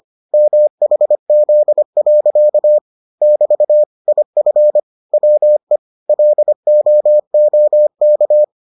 Morse code, heard as a keyed tone, about 25 wpm, 600 Hz. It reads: MHZ. <BT> IF WE LOOK